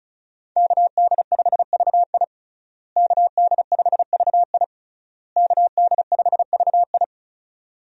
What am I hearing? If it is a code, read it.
Morse code, heard as a keyed tone, about 34 wpm, 700 Hz. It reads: KD5VI KD5VI KD5VI
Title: USA Call Signs - 5 Characters - Repeated 3x - 35wpm